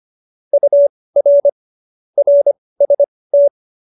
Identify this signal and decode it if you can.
Morse code, heard as a keyed tone, about 25 wpm, spaced as for 18 wpm, 570 Hz. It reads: UR RST
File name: qso222 18